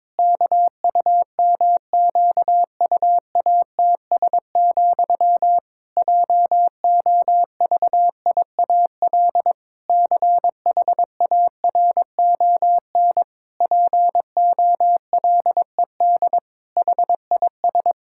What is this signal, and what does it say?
Morse code, heard as a keyed tone, about 22 wpm, 700 Hz. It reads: KUMQUATS, JOVIAL CHARON POLED HIS
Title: Pangram Challenge v2 - 22wpm